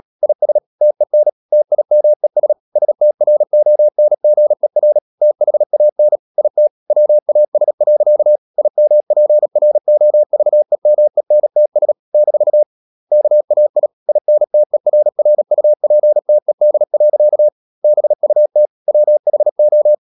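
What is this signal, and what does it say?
Morse code, heard as a keyed tone, about 37 wpm, 600 Hz. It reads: IS TEN TIMES STRONGER THAN IT WAS. IMPROVEMENTS - KAI INTERRUPTED. BUT WHO
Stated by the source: Morse code practice at 35–39 wpm sending English text of Ken Follett's novel Never, from CW med Gnister.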